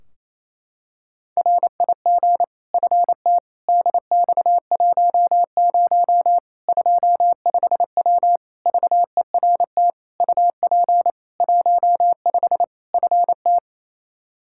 Morse code, heard as a keyed tone, about 28 wpm, 700 Hz. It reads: RIG FT DX10 25W VERT UP 15 FT
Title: K1USN SST QSO Practice 28wpm